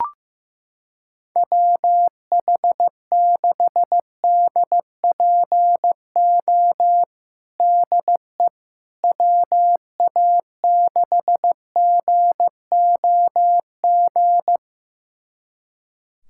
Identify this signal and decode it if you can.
Morse code, heard as a keyed tone, about 15 wpm, 700 Hz. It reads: WH6DPO DE WA6GOG
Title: USA Callsign DE Callsign 15wpm